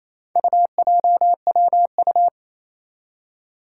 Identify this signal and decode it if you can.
Morse code, heard as a keyed tone, about 28 wpm, 700 Hz. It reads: UJWU
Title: Four Challenging Letter-Number Combinations 28wpm